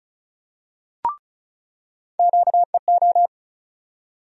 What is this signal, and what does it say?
Morse code, heard as a keyed tone, about 36 wpm, 700 Hz. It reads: QEO
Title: USA Call Sign Postfixes - 3 Letters 35wpm